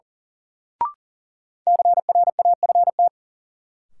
Morse code, heard as a keyed tone, about 37 wpm, 700 Hz. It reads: CRAFT
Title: Sentences from Top 1000 Words - Review Part 6 - New Word Drill - 40wpm